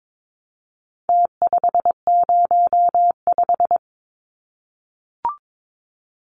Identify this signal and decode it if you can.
Morse code, heard as a keyed tone, about 22 wpm, 700 Hz. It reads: T505
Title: Numbers - Contest Style - 4-Digit 22wpm